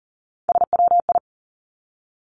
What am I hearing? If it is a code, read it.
Morse code, heard as a keyed tone, about 40 wpm, 700 Hz. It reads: SWI
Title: Three Challenging Letter-Number Combinations 40wpm